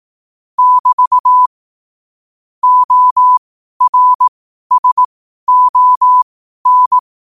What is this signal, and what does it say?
Morse code, heard as a keyed tone, about 18 wpm, spaced as for 12 wpm, 1000 Hz. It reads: <BT> ORSON
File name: qotdfiles / 12wpm